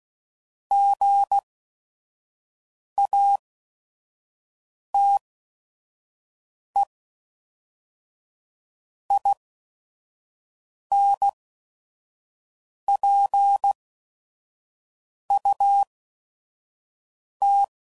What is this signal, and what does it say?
Morse code, heard as a keyed tone, about 16 wpm, spaced as for 5 wpm, 800 Hz. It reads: GATE INPUT
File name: CW Code Groups @5WPM / Technical Subjects / OR Gate